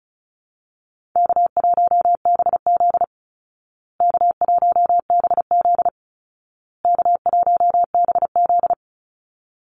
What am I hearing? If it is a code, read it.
Morse code, heard as a keyed tone, about 34 wpm, 700 Hz. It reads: K1BZ K1BZ K1BZ
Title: Call Signs Repeated 2x - 35wpm